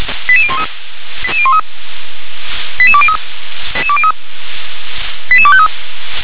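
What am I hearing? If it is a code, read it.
Five-tone selective call (ZVEI), heard as quick runs of five tones, tones of about 70 ms. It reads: 88013 88203 88253
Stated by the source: ZVEI five-tone call received by SDR.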